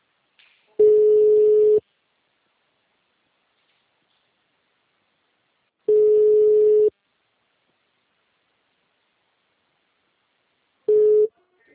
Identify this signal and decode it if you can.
Ringback tone (Europe, 425 Hz).